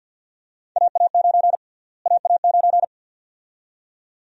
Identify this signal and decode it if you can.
Morse code, heard as a keyed tone, about 53 wpm, 700 Hz. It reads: AA9 AA9
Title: USA Call Sign Prefixes - Repeated 2x - 50wpm